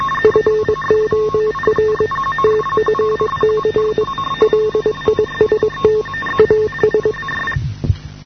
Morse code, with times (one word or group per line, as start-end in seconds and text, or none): 0.2-2.1 FOR
2.4-4.0 TFC
4.4-6.0 LIST
6.4-7.1 AS